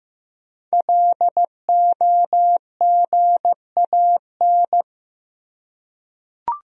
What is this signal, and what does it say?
Morse code, heard as a keyed tone, about 15 wpm, 700 Hz. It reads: LOGAN
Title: USA Common Names - 15wpm